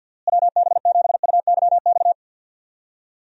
Morse code, heard as a keyed tone, about 51 wpm, 700 Hz. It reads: WB7UYX
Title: USA Call Signs - 6 Characters - Repeated 3x - 50wpm